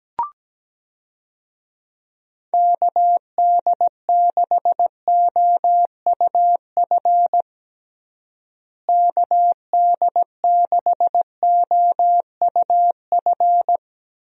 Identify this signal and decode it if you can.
Morse code, heard as a keyed tone, about 17 wpm, 700 Hz. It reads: KD6OUF KD6OUF
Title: USA Call Signs - 6 Characters - Repeated 3x - 17wpm